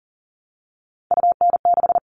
Morse code, heard as a keyed tone, about 40 wpm, 700 Hz. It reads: UN6
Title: All Cut Numbers - 3-Digit 40wpm